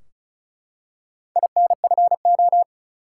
Morse code, heard as a keyed tone, about 36 wpm, 700 Hz. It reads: INFO
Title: Top 180 QSO Elements 35wpm